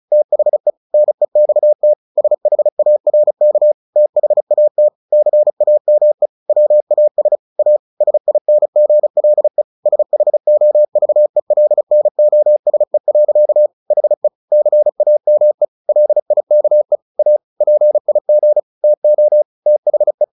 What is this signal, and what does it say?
Morse code, heard as a keyed tone, about 34 wpm, 600 Hz. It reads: THE NEXT SHARK THAT CAME WAS A SINGLE SHOVELNOSE. HE CAME LIKE A PIG TO THE
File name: the old man and the sea 2 35wpm